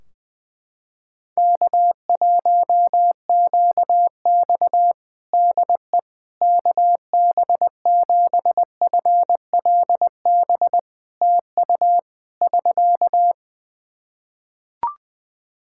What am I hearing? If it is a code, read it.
Morse code, heard as a keyed tone, about 20 wpm, 700 Hz. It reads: K1QX DE KB7FLB TU <SK>
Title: Practice QSO - Exchange 4 - Ending - 20wpm